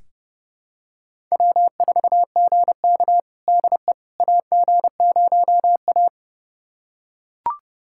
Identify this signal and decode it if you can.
Morse code, heard as a keyed tone, about 30 wpm, 700 Hz. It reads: W4GK DE AG0A
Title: USA Callsign DE Callsign - 4 Characters - 30wpm